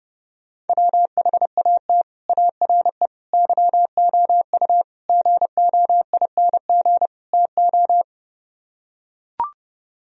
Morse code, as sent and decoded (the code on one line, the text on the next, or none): .-- .... .- - / .- .-. / -.-- --- ..- / --. --- .. -. --. / - ---
WHAT ARE YOU GOING TO